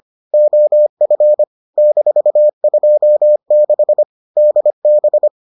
Morse code, heard as a keyed tone, about 25 wpm, 600 Hz. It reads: OF -26 DB